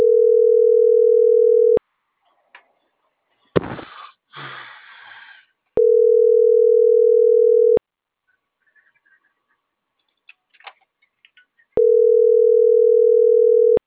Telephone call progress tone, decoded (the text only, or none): ringback tone